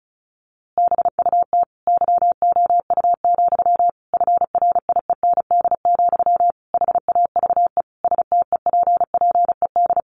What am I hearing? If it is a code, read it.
Morse code, heard as a keyed tone, about 34 wpm, 700 Hz. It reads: BUT YOU, FRIEND, HAVE STEPPED
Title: A Sound Of Thunder 35wpm